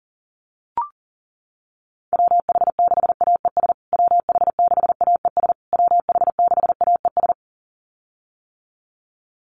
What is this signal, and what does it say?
Morse code, heard as a keyed tone, about 40 wpm, 700 Hz. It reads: WH6AES WH6AES WH6AES
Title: USA Call Signs Repeated 3x 40wpm